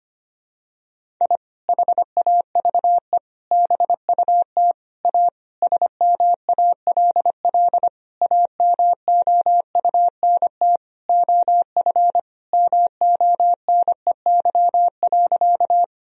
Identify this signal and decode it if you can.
Morse code, heard as a keyed tone, about 25 wpm, 700 Hz. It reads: I HAVE BUT A SMALL AMOUNT OF MONEY.
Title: Sentences from Top 1000 Words - Review Part 4 - 25wpm